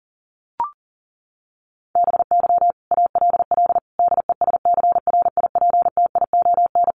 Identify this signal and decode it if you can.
Morse code, heard as a keyed tone, about 40 wpm, 700 Hz. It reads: BY ALL DESCRIPTION
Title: Sentences from Top 1300 Words 40wpm